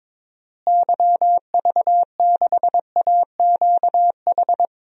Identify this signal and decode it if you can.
Morse code, heard as a keyed tone, about 22 wpm, 700 Hz. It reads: YV6AQH
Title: DXCC-100 Call Signs - 6 Characters - Repeated 2x - 22wpm